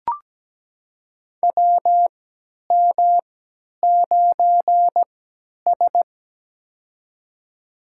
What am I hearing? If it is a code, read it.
Morse code, heard as a keyed tone, about 17 wpm, spaced as for 10 wpm, 700 Hz. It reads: WM9S
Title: USA Call Signs - 4 Characters - 3x Character Spacing - 17wpm